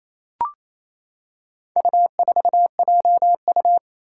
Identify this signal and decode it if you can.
Morse code, heard as a keyed tone, about 27 wpm, 700 Hz. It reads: U4JU
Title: Four Challenging Letter-Number Combinations 28wpm